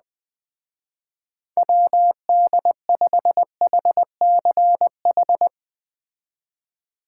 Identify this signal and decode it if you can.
Morse code, heard as a keyed tone, about 20 wpm, 700 Hz. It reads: WD5HCH
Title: Speed Racing - USA 2x3 Call Signs 17wpm